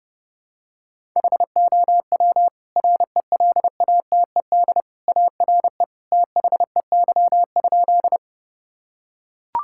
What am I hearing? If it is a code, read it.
Morse code, heard as a keyed tone, about 30 wpm, 700 Hz. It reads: HOW RELATED ARE THEY?
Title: Sentences from Top 1000 Words - Review Part 5 - 30wpm